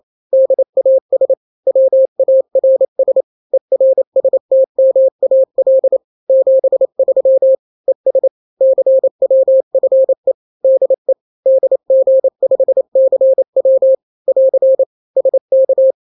Morse code, heard as a keyed tone, about 28 wpm, 530 Hz. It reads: DAS WARS ERSTMAL 73 ES CWFE DE DG5CW <AR> SK
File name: daily / 27wpm